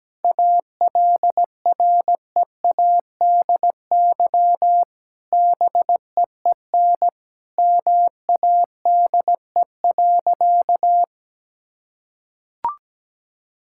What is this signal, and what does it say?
Morse code, heard as a keyed tone, about 17 wpm, 700 Hz. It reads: ALREADY BEEN MADE.
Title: Sentences from Top 1000 Words - Review Part 5 - 17wpm